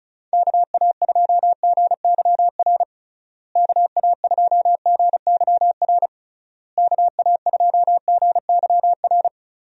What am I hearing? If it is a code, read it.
Morse code, heard as a keyed tone, about 34 wpm, 700 Hz. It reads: KA2GYR KA2GYR KA2GYR
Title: USA Call Signs - 6 Characters - Repeated 3x - 35wpm